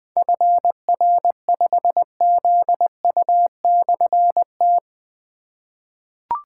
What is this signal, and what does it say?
Morse code, heard as a keyed tone, about 20 wpm, 700 Hz. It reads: FR5ZU/T
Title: Difficult Call Signs 20wpm